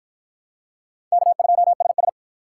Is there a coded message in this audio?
Yes: Morse code from 1.1 to 2.1 s, heard as a keyed tone, about 46 wpm, 700 Hz. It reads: K2SS